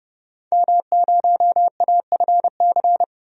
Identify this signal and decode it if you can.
Morse code, heard as a keyed tone, about 32 wpm, 700 Hz. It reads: M0AFC
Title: DXCC-100 Call Signs - 5 Characters - 30wpm